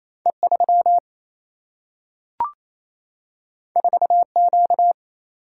Morse code, heard as a keyed tone, about 28 wpm, 700 Hz. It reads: E3 4Q
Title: Single-Letter Single-Number 28wpm